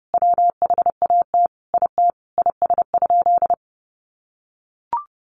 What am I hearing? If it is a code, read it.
Morse code, heard as a keyed tone, about 30 wpm, 700 Hz. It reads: WHAT IT IS?
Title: Sentences from Top 100 Words 30wpm